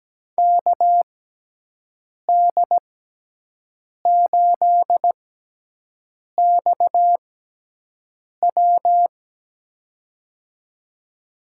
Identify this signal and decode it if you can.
Morse code, heard as a keyed tone, about 17 wpm, spaced as for 6 wpm, 700 Hz. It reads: KD8XW
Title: USA Call Signs - 5 Characters - 6x Character Spacing - 17wpm